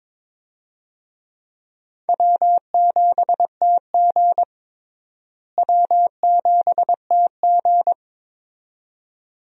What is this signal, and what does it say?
Morse code, heard as a keyed tone, about 22 wpm, 700 Hz. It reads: W7TG W7TG
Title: USA Call Signs - 4 Characters - Repeated 2x - 22wpm